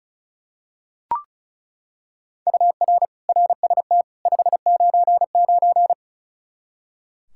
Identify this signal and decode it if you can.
Morse code, heard as a keyed tone, about 36 wpm, 700 Hz. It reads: UR RST 599
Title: K1USN SST QSO Practice 35wpm